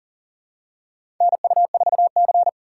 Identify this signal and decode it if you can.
Morse code, heard as a keyed tone, about 40 wpm, 700 Hz. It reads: NU4C
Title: USA Call Signs - 4 Characters - 6x Character Spacing - 40wpm